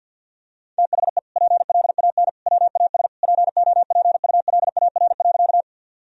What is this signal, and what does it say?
Morse code, heard as a keyed tone, about 50 wpm, 700 Hz. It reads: THE PLAN WAS POPULAR.
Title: Sentences from Top 1000 Words - Review Part 3 - 50wpm